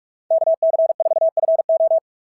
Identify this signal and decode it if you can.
Morse code, heard as a keyed tone, about 41 wpm, 680 Hz. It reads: KC4FO
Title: SOTA QSO Practice 45wpm